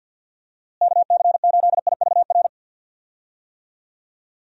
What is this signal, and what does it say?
Morse code, heard as a keyed tone, about 49 wpm, 700 Hz. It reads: KC8IVR
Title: USA Call Signs - 6 Characters - Repeated 2x - 50wpm